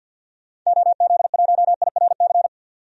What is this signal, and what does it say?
Morse code, heard as a keyed tone, about 49 wpm, 700 Hz. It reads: OZ1IRC
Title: DXCC-100 Call Signs - 6 Characters - 50wpm